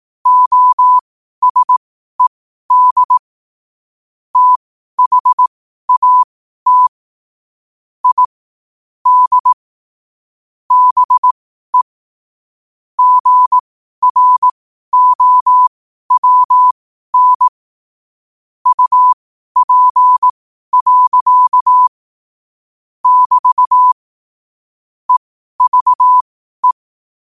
Morse code, heard as a keyed tone, about 18 wpm, spaced as for 12 wpm, 1000 Hz. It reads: OSED THAT I D BE GROWN UP. <BT> EVE
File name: qotdfiles / 12wpm